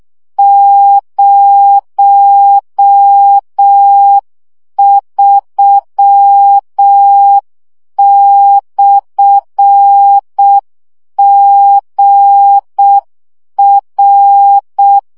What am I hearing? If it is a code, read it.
Morse code, heard as a keyed tone, about 6 wpm, 800 Hz. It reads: 03/GR
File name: morze vizsga 2011-3 5